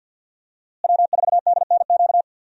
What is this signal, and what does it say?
Morse code, heard as a keyed tone, about 52 wpm, 700 Hz. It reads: W4DNQ
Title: USA Call Signs - 5 Characters - Repeated 3x - 50wpm